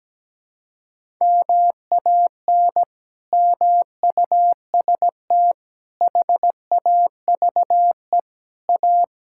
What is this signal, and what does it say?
Morse code, heard as a keyed tone, about 17 wpm, 700 Hz. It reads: MAN MUST HAVE A
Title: Sentences from Top 1000 Words - Review Part 6 - 17wpm